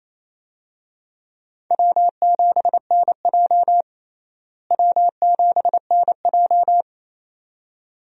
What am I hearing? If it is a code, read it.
Morse code, heard as a keyed tone, about 28 wpm, 700 Hz. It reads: W7NJ W7NJ
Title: USA Call Signs - 4 Characters - Repeated 2x - 28wpm